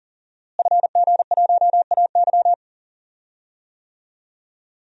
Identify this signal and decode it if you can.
Morse code, heard as a keyed tone, about 40 wpm, 700 Hz. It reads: FG1AY
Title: DXCC-100 Call Signs - 5 Characters - Repeated 3x - 40wpm